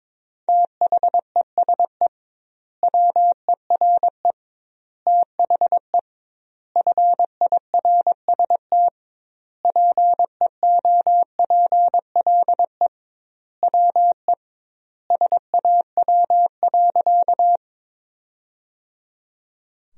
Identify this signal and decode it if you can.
Morse code, heard as a keyed tone, about 22 wpm, 700 Hz. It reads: THESE WERE THE FIRST PEOPLE WE SAW.
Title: 2x Word-Spacing - Sentences from Top 300 Words - 22wpm